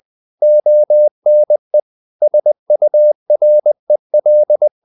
Morse code, heard as a keyed tone, about 20 wpm, 600 Hz. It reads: ONE SUREL
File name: the old man and the sea part 2 20wpm